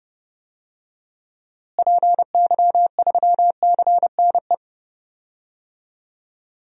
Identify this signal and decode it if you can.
Morse code, heard as a keyed tone, about 30 wpm, 700 Hz. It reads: PY3CNE
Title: DXCC-100 Call Signs - 6 Characters - Repeated 3x - 30wpm